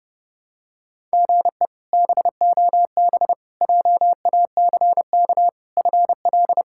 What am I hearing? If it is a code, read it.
Morse code, heard as a keyed tone, about 32 wpm, 700 Hz. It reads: GE BOB JACK FL
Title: K1USN SST QSO Practice - 30wpm